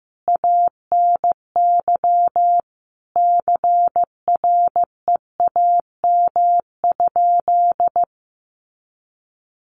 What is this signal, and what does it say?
Morse code, heard as a keyed tone, about 14 wpm, 700 Hz. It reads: ANY CREAM?
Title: Sentences from Top 1500 Words 15wpm